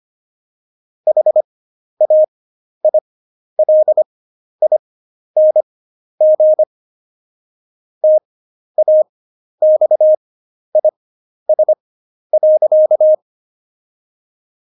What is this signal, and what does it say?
Morse code, heard as a keyed tone, about 25 wpm, spaced as for 11 wpm, 620 Hz. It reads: HAILING TAXIS.
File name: Jokes-25-char-10-eff